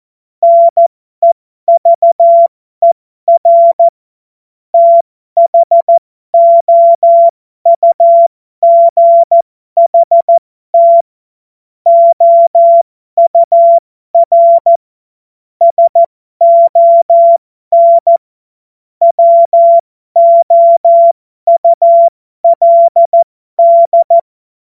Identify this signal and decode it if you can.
Morse code, heard as a keyed tone, about 14 wpm, spaced as for 12 wpm, 680 Hz. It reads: NEVER THOUGHT OUR SON WOULD
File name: Morsekurs / Witze